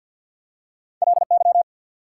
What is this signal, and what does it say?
Morse code, heard as a keyed tone, about 54 wpm, 690 Hz. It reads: LY
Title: Prefixes and Suffixes 50wpm